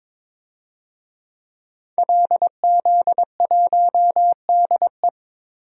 Morse code, heard as a keyed tone, about 21 wpm, 700 Hz. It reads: LZ1DE
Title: DXCC-100 Call Signs - 5 Characters - 22wpm